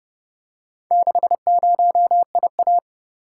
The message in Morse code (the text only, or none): 60IA